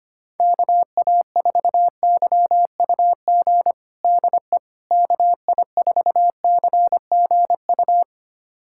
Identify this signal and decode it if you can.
Morse code, heard as a keyed tone, about 25 wpm, 700 Hz. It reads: KA4YUG DE KI4CGU
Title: USA Callsign DE Callsign - 6 Characters - 25wpm